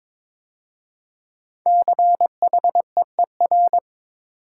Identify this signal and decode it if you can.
Morse code, heard as a keyed tone, about 22 wpm, 700 Hz. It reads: CHEER
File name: 5 Letter Words - First 3 Letters Given - 22wpm